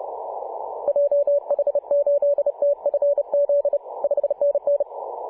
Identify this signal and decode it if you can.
Morse code, heard as a keyed tone, about 30 wpm, 570 Hz. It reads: JH8TFZ 5NN
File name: posts / mar-25-2226z-c21eu-21-020mhz-cw